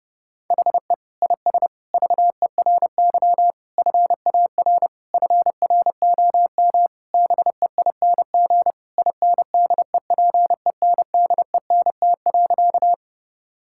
Morse code, heard as a keyed tone, about 29 wpm, 700 Hz. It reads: HE IS VERY FAR FROM BEING INDEPENDENT.
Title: Sentences from Top 1000 Words - Review Part 6 - 30wpm